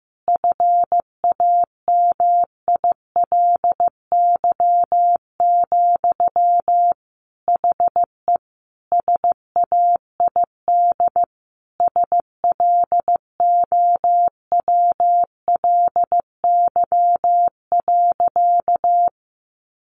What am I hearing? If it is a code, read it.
Morse code, heard as a keyed tone, about 15 wpm, 700 Hz. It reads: FAMILY, HE SAID SLOWLY.
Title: Sentences from Top 1100 Words 15wpm